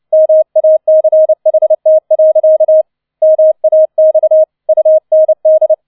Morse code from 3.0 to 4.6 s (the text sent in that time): MAX